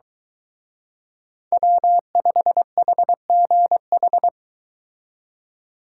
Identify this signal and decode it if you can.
Morse code, heard as a keyed tone, about 23 wpm, 700 Hz. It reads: W5HGH